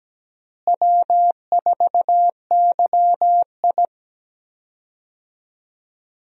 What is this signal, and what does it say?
Morse code, heard as a keyed tone, about 17 wpm, 700 Hz. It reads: W4YI
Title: USA Call Signs - 4 Characters - Repeated 2x - 17wpm